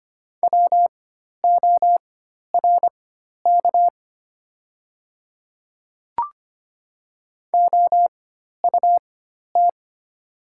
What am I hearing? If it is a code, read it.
Morse code, heard as a keyed tone, about 25 wpm, spaced as for 12 wpm, 690 Hz. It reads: WORK OUT
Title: 4x Character Spacing - Top 300 Words - 25wpm